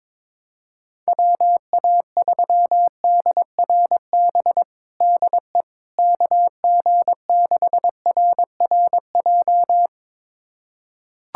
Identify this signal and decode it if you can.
Morse code, heard as a keyed tone, about 22 wpm, 700 Hz. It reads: WA3DRB DE KG6RRJ
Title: USA Callsign DE Callsign - 6 Characters - 22wpm